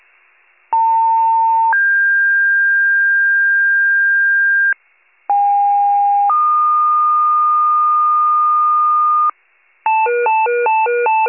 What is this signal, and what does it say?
Two-tone page, heard as a one-second tone, then a three-second tone, about 900 then 1600 Hz, about 810 then 1220 Hz, with noise. It is heard from 0.7 to 9.3 s.
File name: pager / 78 RESCUE DUTY / wcalert